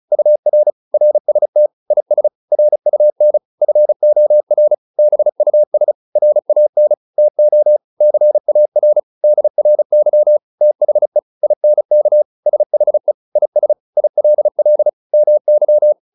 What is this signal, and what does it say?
Morse code, heard as a keyed tone, about 34 wpm, 600 Hz. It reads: UR RST IS RUN FOR BUS RAN TO CAR DRY THE INK SHE IS ILL MY